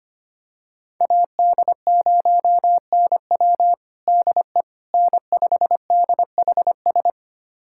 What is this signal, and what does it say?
Morse code, heard as a keyed tone, about 25 wpm, 700 Hz. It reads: AD0NW DE N5DHS
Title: USA Callsign DE Callsign - 5 Characters - 25wpm